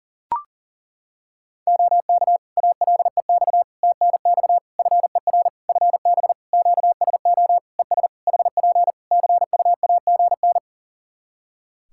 Morse code, heard as a keyed tone, about 40 wpm, 700 Hz. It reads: OK ALEX TNX FER FB QSO ES HP CUAGN
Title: Practice QSO - Exchange 4 - Ending - 40wpm